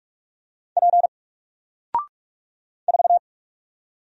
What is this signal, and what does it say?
Morse code, heard as a keyed tone, about 44 wpm, 700 Hz. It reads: P 4
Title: Single Character (Letter:Number:Punctuation:Prosign) 45wpm